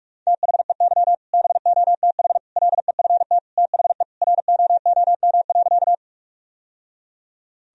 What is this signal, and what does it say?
Morse code, heard as a keyed tone, about 47 wpm, 700 Hz. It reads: THEY BOTH LEFT THE ROOM.
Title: Sentences from Top 1000 Words - Review Part 1 - 45wpm